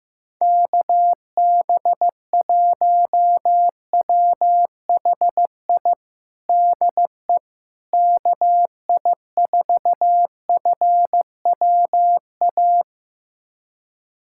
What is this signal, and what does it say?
Morse code, heard as a keyed tone, about 15 wpm, 700 Hz. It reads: KB1WHI DE KI4FWA
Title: USA Callsign DE Callsign - 6 Characters - 15wpm